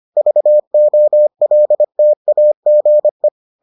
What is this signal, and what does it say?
Morse code, heard as a keyed tone, about 25 wpm, 600 Hz. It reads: VOLTAGE